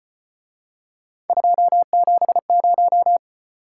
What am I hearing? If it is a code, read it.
Morse code, heard as a keyed tone, about 34 wpm, 700 Hz. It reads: 270